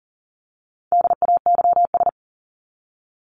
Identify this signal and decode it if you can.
Morse code, heard as a keyed tone, about 37 wpm, 700 Hz. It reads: DAYS
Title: ES:S Suffix - Top 100 Words with Base Words 40wpm